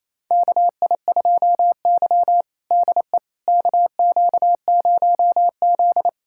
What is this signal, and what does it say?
Morse code, heard as a keyed tone, about 28 wpm, 700 Hz. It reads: KI2Y DE KQ0Z
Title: USA Callsign DE Callsign - 4 Characters - 28wpm